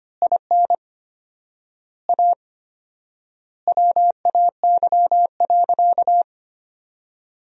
Morse code, heard as a keyed tone, about 25 wpm, 700 Hz. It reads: IN A WAY.
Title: 4x Word-Spacing - Sentences from Top 500 Words - 25wpm